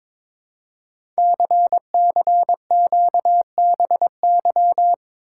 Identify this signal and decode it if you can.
Morse code, heard as a keyed tone, about 22 wpm, 700 Hz. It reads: CCQBY
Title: Code Groups - 5 Letters - 22wpm